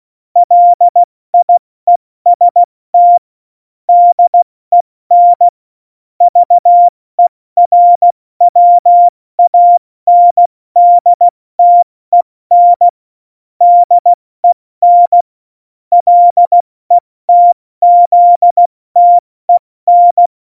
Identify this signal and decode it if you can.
Morse code, heard as a keyed tone, about 16 wpm, spaced as for 14 wpm, 700 Hz. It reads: LIEST DEN VERWANDTEN DEN LETZTEN
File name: Morsekurs / Witze / Witz 18. Februar 2020, 70 F80